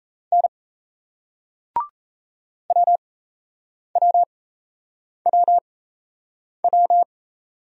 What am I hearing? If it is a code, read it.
Morse code, heard as a keyed tone, about 39 wpm, spaced as for 8 wpm, 700 Hz. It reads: N WWWW